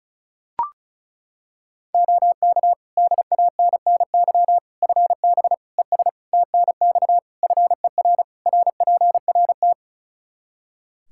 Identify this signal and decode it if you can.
Morse code, heard as a keyed tone, about 36 wpm, 700 Hz. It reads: OK DANNY FB ES TNX FER RPRT
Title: Practice QSO - Exchange 2 - Info - 35wpm